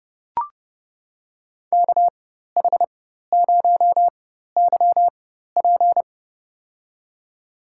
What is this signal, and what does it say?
Morse code, heard as a keyed tone, about 30 wpm, spaced as for 14 wpm, 700 Hz. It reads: KH0YP
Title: USA Call Signs - 5 Characters - 4x Character Spacing - 30wpm